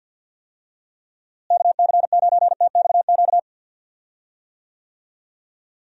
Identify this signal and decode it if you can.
Morse code, heard as a keyed tone, about 51 wpm, 700 Hz. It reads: KC9TXQ
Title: USA Call Signs - 6 Characters - Repeated 2x - 50wpm